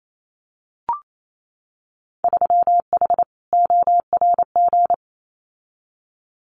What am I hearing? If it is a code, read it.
Morse code, heard as a keyed tone, about 28 wpm, 700 Hz. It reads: 3H ORG
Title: Winter Field Day Exchanges Encore 28wpm